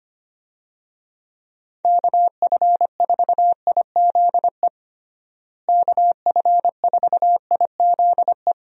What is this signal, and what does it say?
Morse code, heard as a keyed tone, about 25 wpm, 700 Hz. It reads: KF4IZE KF4IZE
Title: USA Call Signs - 6 Characters - Repeated 2x - 25wpm